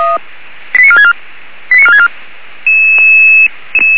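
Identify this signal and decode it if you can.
Five-tone selective call (ZVEI), heard as quick runs of five tones, tones of about 70 ms. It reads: 89464 89464
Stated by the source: ZVEI five-tone selective call, address 89273, received by SDR.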